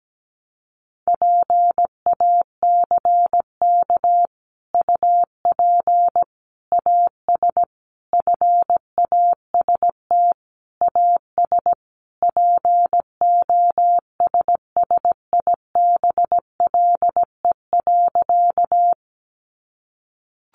Morse code, heard as a keyed tone, about 17 wpm, 700 Hz. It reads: PACK UP AS FAST AS POSSIBLE.
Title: Sentences from Top 1300 Words 17wpm